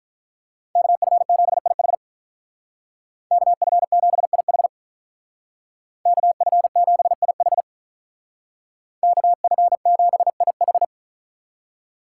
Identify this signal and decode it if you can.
Morse code, heard as a keyed tone, about 43 wpm, 700 Hz. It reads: KF7IH KF7IH KF7IH KF7IH
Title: Speed Racing - USA 2x2 Call Signs 35wpm